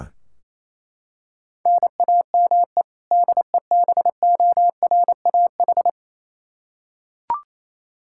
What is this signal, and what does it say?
Morse code, heard as a keyed tone, about 28 wpm, 700 Hz. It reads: NAME DEBORAH
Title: Practice QSO - Exchange 1 - Beginning - 28wpm